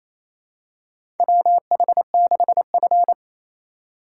Morse code, heard as a keyed tone, about 28 wpm, 700 Hz. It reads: WH6F
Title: USA Call Signs - 4x Character Spacing - 28wpm